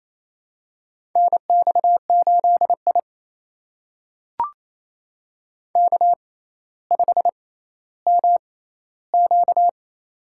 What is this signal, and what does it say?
Morse code, heard as a keyed tone, about 29 wpm, spaced as for 15 wpm, 700 Hz. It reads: NX8I K5MQ
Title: USA Call Signs - 4 Characters - 6x Character Spacing - 28wpm